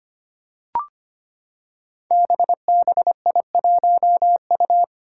Morse code, heard as a keyed tone, about 25 wpm, 700 Hz. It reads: BBI1U
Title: Five Challenging Letter-Number Combinations 25wpm